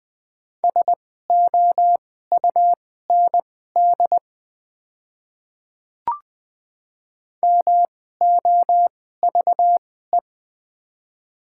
Morse code, heard as a keyed tone, about 20 wpm, spaced as for 14 wpm, 700 Hz. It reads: SOUND MOVE
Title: 2x Character Spacing - Top 300 Words - 20wpm